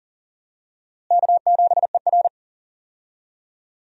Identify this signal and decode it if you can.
Morse code, heard as a keyed tone, about 40 wpm, 700 Hz. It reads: K7ER
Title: USA Call Signs - 4 Characters - Repeated 2x - 40wpm